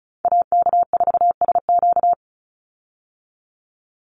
Morse code, heard as a keyed tone, about 34 wpm, 700 Hz. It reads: AK4SQ
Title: Call Signs Repeated 2x - 35wpm